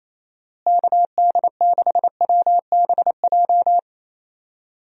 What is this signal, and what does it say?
Morse code, heard as a keyed tone, about 28 wpm, 700 Hz. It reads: KD6WBJ
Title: USA Call Signs - 6 Characters - Repeated 3x - 28wpm